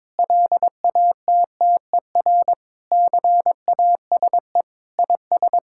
Morse code, heard as a keyed tone, about 22 wpm, 700 Hz. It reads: LATTER CASE IS
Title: Sentences from Top 1600 Words 22wpm